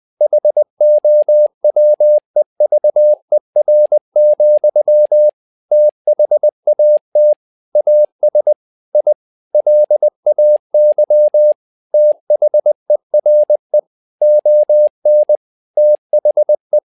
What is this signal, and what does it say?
Morse code, heard as a keyed tone, about 20 wpm, 600 Hz. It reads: HOWEVER, THAT AS I LAY THERE ON THE
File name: learning center / tower trouble / tower trouble 20wpm